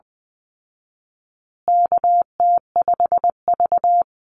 Morse code, heard as a keyed tone, about 20 wpm, 700 Hz. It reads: KT5V
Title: Call Signs Repeated 2x - 20wpm